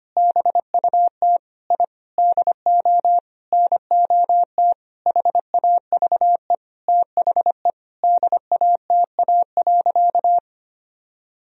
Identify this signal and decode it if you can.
Morse code, heard as a keyed tone, about 25 wpm, 700 Hz. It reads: BUT I DO NOT HAVE THE DATA.
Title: Sentences from Top 1000 Words - Review Part 3 - 25wpm